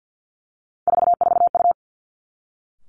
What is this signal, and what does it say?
Morse code, heard as a keyed tone, about 49 wpm, 700 Hz. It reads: 44U
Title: Three Challenging Letter-Number Combinations 50wpm